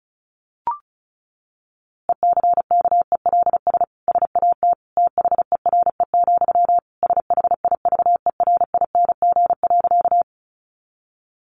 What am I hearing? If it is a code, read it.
Morse code, heard as a keyed tone, about 34 wpm, 700 Hz. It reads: ECKELS SAT THERE, SHIVERING.
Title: A Sound Of Thunder 35wpm